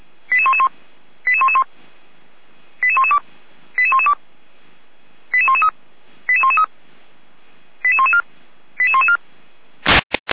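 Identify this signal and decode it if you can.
Five-tone selective call (ZVEI), heard as quick runs of five tones, tones of about 70 ms. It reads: 88111 88111 88112 88112 88113 88113 88114 88114
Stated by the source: ZVEI five-tone call received on an SDR.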